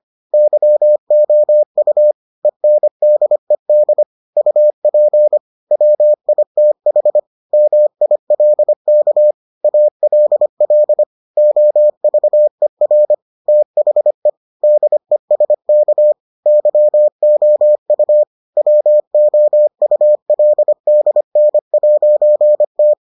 Morse code, heard as a keyed tone, about 25 wpm, 600 Hz. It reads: YOU ENDED UP WITH MILK ALL OVER THE DESK YOU WOULDN'T